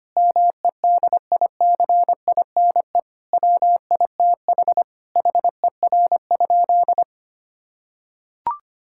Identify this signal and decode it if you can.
Morse code, heard as a keyed tone, about 25 wpm, 700 Hz. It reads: MEDICINE WITH HER?
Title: Sentences from Top 1000 Words - Review Part 6 - 25wpm